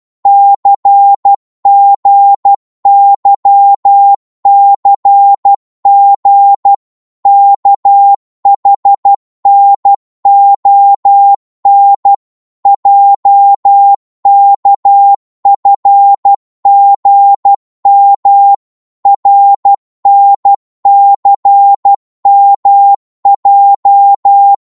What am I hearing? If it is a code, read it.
Morse code, heard as a keyed tone, about 12 wpm, 800 Hz. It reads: CGYCG KHNON JKFGM RNCMJ
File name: Morse / ltr-12wpm